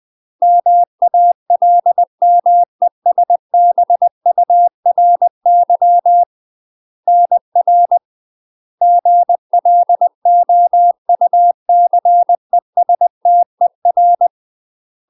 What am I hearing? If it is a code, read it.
Morse code, heard as a keyed tone, about 20 wpm, 700 Hz. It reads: MALMESBURY NR GLOUCESTER
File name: UK MASTER QSOs TRAINING / 20WPM / UK MASTER QSOs 0059